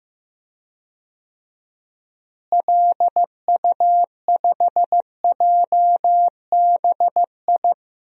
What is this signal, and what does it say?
Morse code, heard as a keyed tone, about 15 wpm, 700 Hz. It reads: LU5JBI
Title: DXCC-100 Call Signs - 6 Characters - Repeated 3x - 15wpm